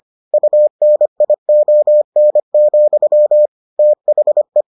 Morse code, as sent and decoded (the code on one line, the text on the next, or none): ..- -. .. --- -. --..-- / - .... .
UNION, THE